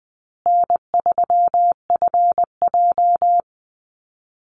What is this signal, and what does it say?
Morse code, heard as a keyed tone, about 20 wpm, 700 Hz. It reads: N3FJ
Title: USA Call Signs Repeated 3x 20wpm